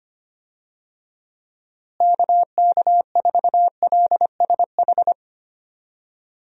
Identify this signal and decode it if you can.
Morse code, heard as a keyed tone, about 25 wpm, 700 Hz. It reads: KK4LSH
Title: USA Call Signs - 6 Characters - Repeated 3x - 25wpm